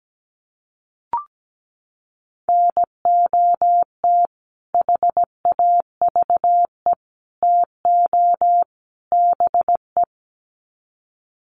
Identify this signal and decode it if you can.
Morse code, heard as a keyed tone, about 17 wpm, 700 Hz. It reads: NOT HAVE TO BE
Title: Sets of 4 Words 17wpm